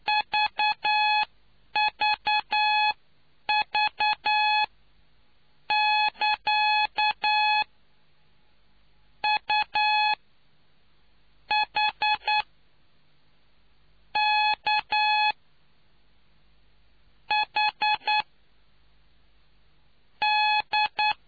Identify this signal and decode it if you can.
Morse code, heard as a keyed tone, about 9 wpm, spaced as for 5 wpm, 850 Hz. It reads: VVV<KA> UHKHD